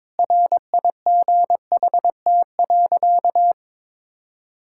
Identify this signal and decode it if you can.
Morse code, heard as a keyed tone, about 22 wpm, 700 Hz. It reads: RIGHT.
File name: practice / 22wpm / Sentences with 3-Word Phrases - 22wpm